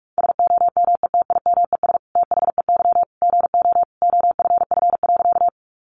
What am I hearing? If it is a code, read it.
Morse code, heard as a keyed tone, about 44 wpm, 700 Hz. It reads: SOMETIMES THEY GO OFF.
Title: Sentences from Top 600 Words 45wpm